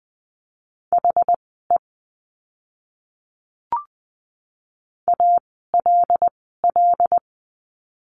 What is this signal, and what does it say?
Morse code, heard as a keyed tone, about 20 wpm, spaced as for 15 wpm, 700 Hz. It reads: HE ALL
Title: 244 Lesson 39f - Letter z - Words 20wpm